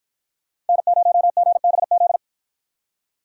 Morse code, heard as a keyed tone, about 53 wpm, 700 Hz. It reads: N0GBZ